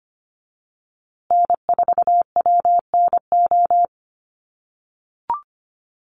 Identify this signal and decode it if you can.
Morse code, heard as a keyed tone, about 25 wpm, 700 Hz. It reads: N4WNO
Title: USA Call Signs Repeated 2x 25wpm